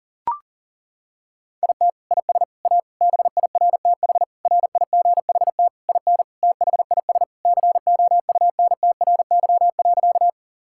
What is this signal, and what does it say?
Morse code, heard as a keyed tone, about 37 wpm, 700 Hz. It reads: IT IS A BIRTH RIGHT IN THIS COUNTRY.
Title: Sentences from Top 1000 Words - Review Part 5 - 40wpm